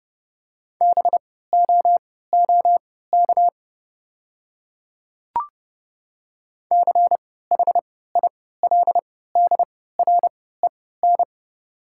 Morse code, heard as a keyed tone, about 32 wpm, spaced as for 17 wpm, 700 Hz. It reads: BOOK CHILDREN
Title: 3x Character Spacing - Top 300 Words - 30wpm